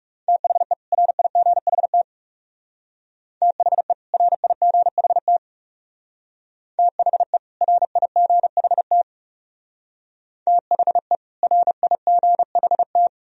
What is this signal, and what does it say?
Morse code, heard as a keyed tone, about 36 wpm, 700 Hz. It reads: THE RIGHT THE RIGHT THE RIGHT THE RIGHT
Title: Speed-Racing - Sets of 2 Words 30wpm